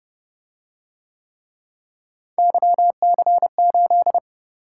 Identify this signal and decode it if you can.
Morse code, heard as a keyed tone, about 29 wpm, 700 Hz. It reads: YC8